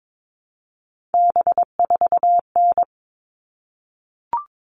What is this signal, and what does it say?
Morse code, heard as a keyed tone, about 22 wpm, 700 Hz. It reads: B4N
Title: All Cut Numbers - 3-Digit 22wpm